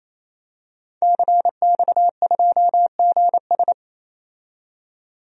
Morse code, heard as a keyed tone, about 28 wpm, 700 Hz. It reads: CX2GS
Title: DXCC-100 Call Signs - 5 Characters - 28wpm